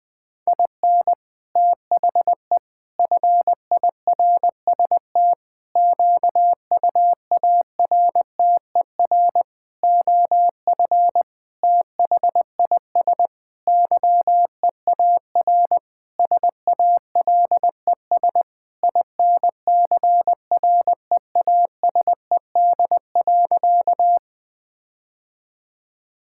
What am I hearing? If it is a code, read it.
Morse code, heard as a keyed tone, about 20 wpm, 700 Hz. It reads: IN THE FIRST QUARTER OF THIS YEAR SALES INCREASED.
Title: Sentences with 7-Word Phrases - Spoken Beginning - 20wpm